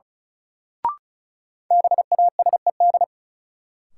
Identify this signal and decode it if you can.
Morse code, heard as a keyed tone, about 36 wpm, 700 Hz. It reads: BASED
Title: Sets of 2 Words - New Word Drill - 35wpm